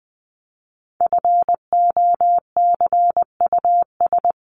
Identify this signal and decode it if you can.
Morse code, heard as a keyed tone, about 20 wpm, 700 Hz. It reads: FOCUS
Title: Top 501-750 Words 20wpm